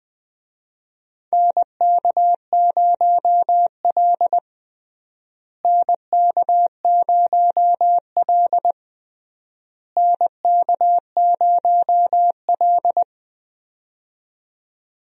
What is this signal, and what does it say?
Morse code, heard as a keyed tone, about 21 wpm, 700 Hz. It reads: NK0L NK0L NK0L
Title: USA Call Signs - 4 Characters - Repeated 3x - 20wpm